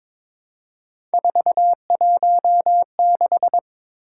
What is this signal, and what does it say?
Morse code, heard as a keyed tone, about 22 wpm, 700 Hz. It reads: 416
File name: practice / 22wpm / Three Letter-Number Combinations - 22wpm